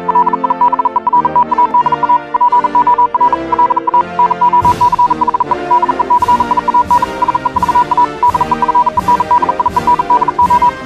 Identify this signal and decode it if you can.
Morse code, heard as a keyed tone, about 40 wpm, 980 Hz, with noise. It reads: LIBERTAD A POR LA NOCHE DECIAN SE VA DEFENDIENDO
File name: morse hymne(1)